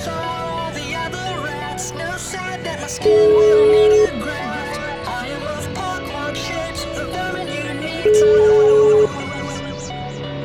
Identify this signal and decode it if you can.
Ringback tone (Europe, 425 Hz), with noise.